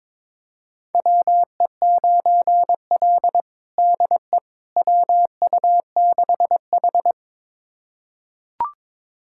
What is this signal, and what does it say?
Morse code, heard as a keyed tone, about 22 wpm, 700 Hz. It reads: WE9L DE WU6H
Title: USA Callsign DE Callsign - 4 Characters - 22wpm